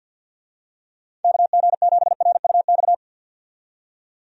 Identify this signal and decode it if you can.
Morse code, heard as a keyed tone, about 50 wpm, 700 Hz. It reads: KG7RUX
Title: USA Call Signs - 6 Characters - 50wpm